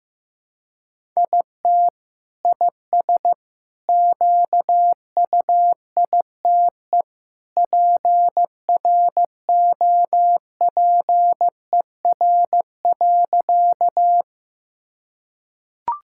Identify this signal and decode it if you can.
Morse code, heard as a keyed tone, about 15 wpm, 700 Hz. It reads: IT IS QUITE PROPER.
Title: Sentences from Top 1000 Words - Review Part 5 - 15wpm